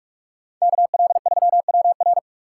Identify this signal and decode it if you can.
Morse code, heard as a keyed tone, about 44 wpm, 690 Hz. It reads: KL3WR
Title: USA Call Signs - 5 Characters - 4x Character Spacing - 45wpm